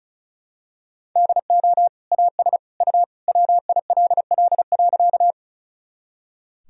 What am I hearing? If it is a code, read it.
Morse code, heard as a keyed tone, about 34 wpm, 700 Hz. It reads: DO AS U WILL.
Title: Sentences from Top 100 Words and Abbreviations 35wpm